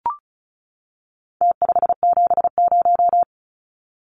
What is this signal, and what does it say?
Morse code, heard as a keyed tone, about 34 wpm, 700 Hz. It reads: T570
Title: Numbers - Contest Style - 4-Digit 35wpm